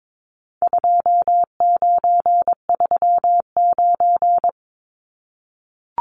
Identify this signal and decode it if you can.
Morse code, heard as a keyed tone, about 22 wpm, 700 Hz. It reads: 2939
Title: Numbers - 4-Digit 22wpm